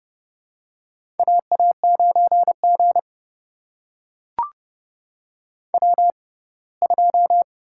Morse code, heard as a keyed tone, about 30 wpm, spaced as for 20 wpm, 700 Hz. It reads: AA9G W2
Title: USA Call Signs - 6x Character Spacing - 30wpm